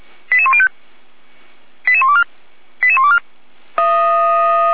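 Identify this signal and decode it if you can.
Five-tone selective call (ZVEI), heard as quick runs of five tones, tones of about 70 ms. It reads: 88105 88124 88124
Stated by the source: ZVEI five-tone selective call received on SDR, Feuerwehr Orlamünde fire brigade.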